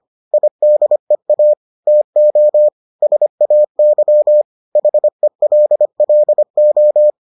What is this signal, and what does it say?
Morse code, heard as a keyed tone, about 25 wpm, 600 Hz. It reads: IDEA TO SAY HELLO